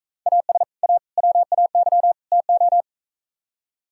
Morse code, heard as a keyed tone, about 43 wpm, 700 Hz. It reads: AS A WAY TO